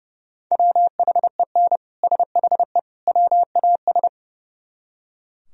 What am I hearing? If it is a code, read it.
Morse code, heard as a keyed tone, about 39 wpm, 700 Hz. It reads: WHEN SHE WAS